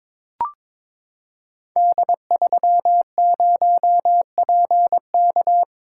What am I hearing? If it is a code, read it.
Morse code, heard as a keyed tone, about 21 wpm, 700 Hz. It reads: D30PK
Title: Code Groups - 5 Letter:Numbers - 22wpm